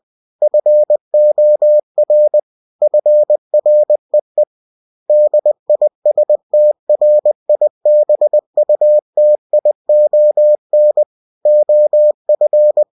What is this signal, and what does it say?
Morse code, heard as a keyed tone, about 20 wpm, 600 Hz. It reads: FOR FREE DISTRIBUTION OF
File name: cw-books / winnie-the-pooh / winnie-20wpm-page-0117